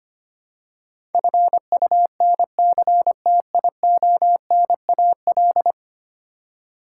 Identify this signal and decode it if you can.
Morse code, heard as a keyed tone, about 25 wpm, 700 Hz. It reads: FUNCTIONAL
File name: 10 Letter Words - First 3 Letters Given - 25wpm